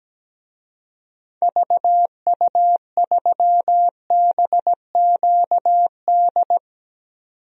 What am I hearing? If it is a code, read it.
Morse code, heard as a keyed tone, about 17 wpm, 700 Hz. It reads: VU3BQD
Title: DXCC-100 Call Signs - 6 Characters - 17wpm